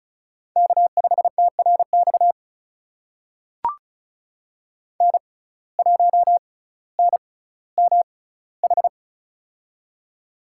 Morse code, heard as a keyed tone, about 36 wpm, spaced as for 18 wpm, 700 Hz. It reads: K5TRX N1NMH
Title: USA Call Signs - 5 Characters - 6x Character Spacing - 35wpm